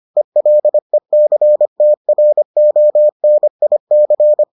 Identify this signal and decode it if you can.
Morse code, heard as a keyed tone, about 25 wpm, 600 Hz. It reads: ELECTRONIC